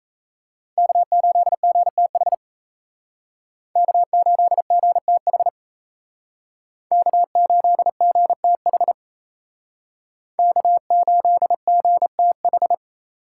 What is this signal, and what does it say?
Morse code, heard as a keyed tone, about 36 wpm, 700 Hz. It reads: K8GTH K8GTH K8GTH K8GTH